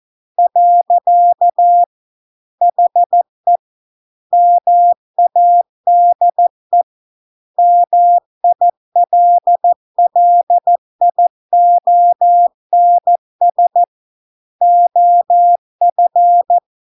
Morse code, heard as a keyed tone, about 14 wpm, 700 Hz. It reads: . HE MADE MILLIONS OF